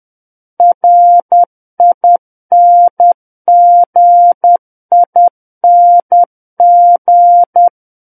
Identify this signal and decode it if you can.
Morse code, heard as a keyed tone, about 10 wpm, 700 Hz. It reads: RINGING
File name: Book One / 01-Chapter 1 - 10 WPM